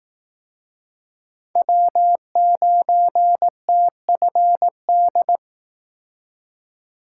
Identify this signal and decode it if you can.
Morse code, heard as a keyed tone, about 19 wpm, 700 Hz. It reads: W9TFD